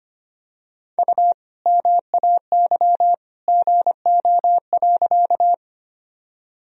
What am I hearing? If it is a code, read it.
Morse code, heard as a keyed tone, about 25 wpm, 700 Hz. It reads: U MAY GO.
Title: Sentences from Top 100 Words and Abbreviations 25wpm